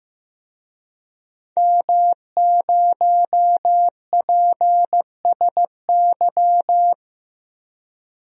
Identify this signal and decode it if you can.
Morse code, heard as a keyed tone, about 16 wpm, 700 Hz. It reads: M0PSY
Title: DXCC-100 Call Signs - 5 Characters - Repeated 2x - 15wpm